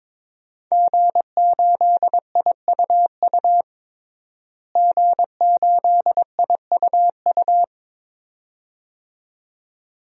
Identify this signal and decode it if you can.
Morse code, heard as a keyed tone, about 21 wpm, 700 Hz. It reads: G8IUU G8IUU
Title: DXCC-100 Call Signs - 5 Characters - Repeated 2x - 22wpm